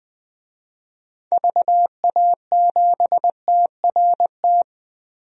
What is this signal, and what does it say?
Morse code, heard as a keyed tone, about 20 wpm, 700 Hz. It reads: VA7TRT
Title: DXCC-100 Call Signs - 6 Characters - 20wpm